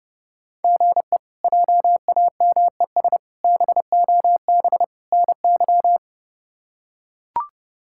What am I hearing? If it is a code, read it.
Morse code, heard as a keyed tone, about 30 wpm, 700 Hz. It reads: GE JAMES BOB NY
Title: K1USN SST QSO Practice - 30wpm